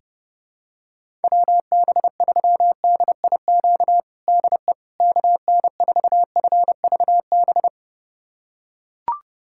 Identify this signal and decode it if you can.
Morse code, heard as a keyed tone, about 30 wpm, 700 Hz. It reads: WB3DIQ DE KN4FVB
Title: USA Callsign DE Callsign - 6 Characters - 30wpm